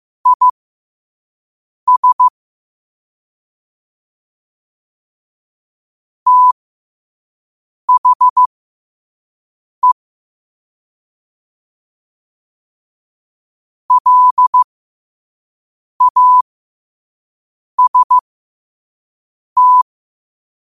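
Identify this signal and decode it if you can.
Morse code, heard as a keyed tone, about 15 wpm, spaced as for 5 wpm, 1000 Hz. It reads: IS THE LAST
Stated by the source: Morse quote-of-the-day practice at 5 wpm.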